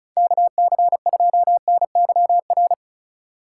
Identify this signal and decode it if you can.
Morse code, heard as a keyed tone, about 36 wpm, 700 Hz. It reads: KC2NYR
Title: USA Call Signs - 8x Character Spacing - 35wpm